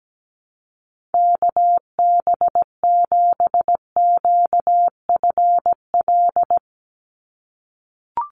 Morse code, heard as a keyed tone, about 17 wpm, 700 Hz. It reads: KB7QFL
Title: USA Call Signs 17wpm